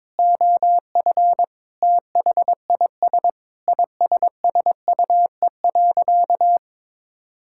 Morse code, heard as a keyed tone, about 21 wpm, 700 Hz. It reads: OF THIS ISSUE.
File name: Sentences with 7-Word Phrases - 22wpm